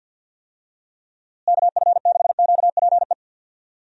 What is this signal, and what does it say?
Morse code, heard as a keyed tone, about 48 wpm, 700 Hz. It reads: KF6QPE
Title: USA Call Signs - 6 Characters - Repeated 3x - 50wpm